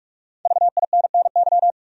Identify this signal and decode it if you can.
Morse code, heard as a keyed tone, about 46 wpm, 700 Hz. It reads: VINNY
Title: USA Common Names - 45wpm